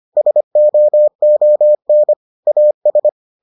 Morse code, heard as a keyed tone, about 25 wpm, 600 Hz. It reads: SOON AS